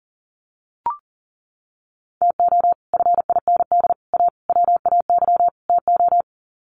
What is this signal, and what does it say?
Morse code, heard as a keyed tone, about 40 wpm, 700 Hz. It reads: TO FIND A WAY TO
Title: Sets of 5 Words 40wpm